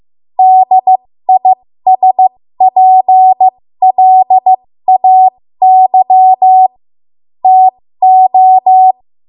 Morse code, heard as a keyed tone, about 15 wpm, spaced as for 13 wpm, 750 Hz. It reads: DISPLAY TO
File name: Morse / 13WPM